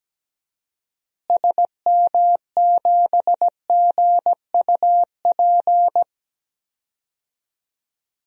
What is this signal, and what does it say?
Morse code, heard as a keyed tone, about 17 wpm, 700 Hz. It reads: SM7GUP
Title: DXCC-100 Call Signs - 6 Characters - Repeated 2x - 17wpm